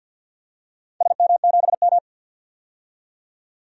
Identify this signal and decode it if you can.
Morse code, heard as a keyed tone, about 48 wpm, 700 Hz. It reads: SM7M
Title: DXCC-100 Call Signs - 4 Characters - 50wpm